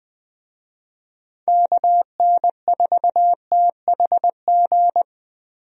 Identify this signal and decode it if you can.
Morse code, heard as a keyed tone, about 20 wpm, 700 Hz. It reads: KN4THG
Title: USA Call Signs - 6 Characters - 20wpm